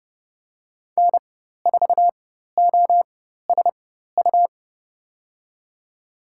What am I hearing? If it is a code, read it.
Morse code, heard as a keyed tone, about 30 wpm, spaced as for 14 wpm, 700 Hz. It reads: N4OSU
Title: USA Call Signs - 5 Characters - 4x Character Spacing - 30wpm